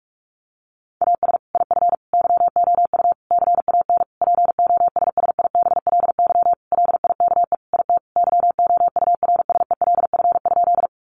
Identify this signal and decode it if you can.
Morse code, heard as a keyed tone, about 44 wpm, 700 Hz. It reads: AS IF YOU CAN POSSIBLY LIKE IT YOURSELF?
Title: Sentences from Top 1300 Words 45wpm